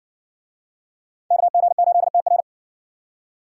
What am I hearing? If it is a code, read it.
Morse code, heard as a keyed tone, about 56 wpm, 700 Hz. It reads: KG8TR